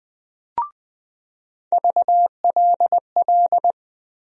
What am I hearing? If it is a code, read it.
Morse code, heard as a keyed tone, about 20 wpm, 700 Hz. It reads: VLL
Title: USA Call Sign Postfixes - 3 Letters 20wpm